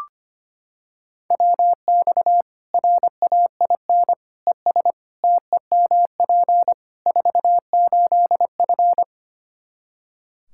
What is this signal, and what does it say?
Morse code, heard as a keyed tone, about 25 wpm, 700 Hz. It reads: WX RAIN ES TEMP 48F
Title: Practice QSO - Exchange 2 - Info - 25wpm